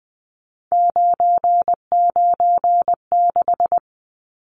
Morse code, heard as a keyed tone, about 20 wpm, 700 Hz. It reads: 996
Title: Numbers - 3-Digit 20wpm